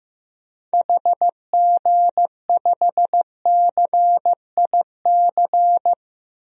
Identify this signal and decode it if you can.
Morse code, heard as a keyed tone, about 15 wpm, 700 Hz. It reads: HG5CIC
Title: DXCC-100 Call Signs - 6 Characters - 15wpm